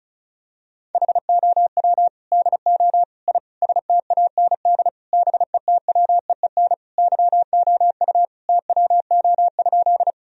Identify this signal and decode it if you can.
Morse code, heard as a keyed tone, about 34 wpm, 700 Hz. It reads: HOW DO I STAND BETWEEN YOU TWO?
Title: Sentences from Top 1000 Words - Review Part 1 - 35wpm